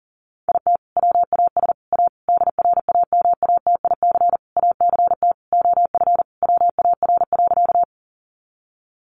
Morse code, heard as a keyed tone, about 40 wpm, 700 Hz. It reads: IT WAS A DRAMATIC ACT OF WAR.
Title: Sentences from Top 1300 Words 40wpm